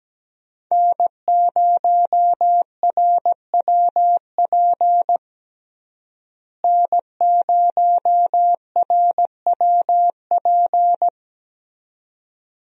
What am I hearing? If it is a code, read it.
Morse code, heard as a keyed tone, about 17 wpm, 700 Hz. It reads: N0RWP N0RWP
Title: USA Call Signs - 5 Characters - Repeated 2x - 17wpm